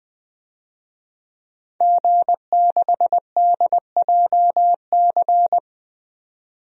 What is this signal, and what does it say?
Morse code, heard as a keyed tone, about 20 wpm, 700 Hz. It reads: G6DJC
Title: DXCC-100 Call Signs - 5 Characters - 20wpm